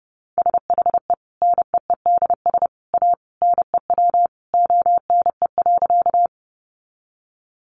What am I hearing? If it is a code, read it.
Morse code, heard as a keyed tone, about 30 wpm, 700 Hz. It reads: SHE NEEDS A NEW ONE.
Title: Additional Easy Phrases 30wpm